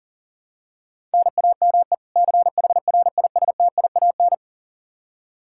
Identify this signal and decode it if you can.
Morse code, heard as a keyed tone, about 38 wpm, 700 Hz. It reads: NAME CHRISTIAN
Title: K1USN SST QSO Practice 40wpm